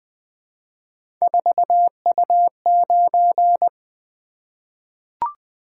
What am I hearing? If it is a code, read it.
Morse code, heard as a keyed tone, about 20 wpm, 700 Hz. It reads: 4U9
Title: Three Letter-Number Combinations - 20wpm